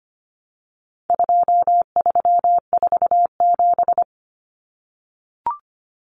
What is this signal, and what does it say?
Morse code, heard as a keyed tone, about 25 wpm, 700 Hz. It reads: 2347
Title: Numbers - 4-Digit 25wpm